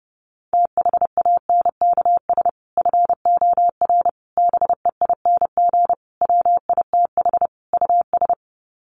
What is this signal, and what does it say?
Morse code, heard as a keyed tone, about 30 wpm, 700 Hz. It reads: THANKS FOR BEING WITH US
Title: Sets of 5 Words 30wpm